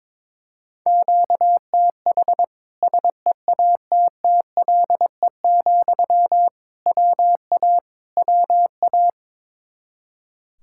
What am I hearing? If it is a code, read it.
Morse code, heard as a keyed tone, about 22 wpm, 700 Hz. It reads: QTH SEATTLE, WA WA
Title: Practice QSO - Exchange 1 - Beginning - 22wpm